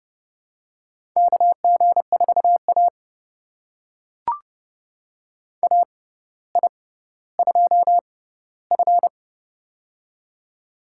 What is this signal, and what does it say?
Morse code, heard as a keyed tone, about 30 wpm, spaced as for 15 wpm, 700 Hz. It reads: KG4A AI2F
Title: USA Call Signs - 4 Characters - 6x Character Spacing - 30wpm